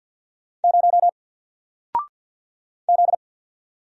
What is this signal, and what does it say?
Morse code, heard as a keyed tone, about 48 wpm, 700 Hz. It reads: 0 Z
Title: Single Letter-Number Alt 50wpm